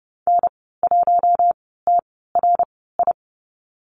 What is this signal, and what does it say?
Morse code, heard as a keyed tone, about 30 wpm, spaced as for 17 wpm, 690 Hz. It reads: N1TRI
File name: Lesson 15g - Number 5 - Call Signs 30wpm